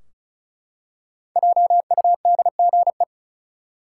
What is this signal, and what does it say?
Morse code, heard as a keyed tone, about 34 wpm, 700 Hz. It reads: JUDGE
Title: Five Letter Words - 35wpm